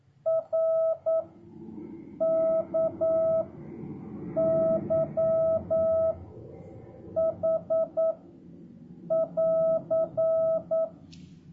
Morse code, heard as a keyed tone, about 9 wpm, 650 Hz. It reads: R K Y H <AR>